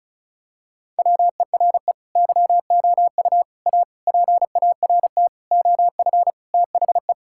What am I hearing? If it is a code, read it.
Morse code, heard as a keyed tone, about 36 wpm, 700 Hz. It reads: WERE YOU A PART OF THE
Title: Sentences from Top 1000 Words - Review Part 6 - 35wpm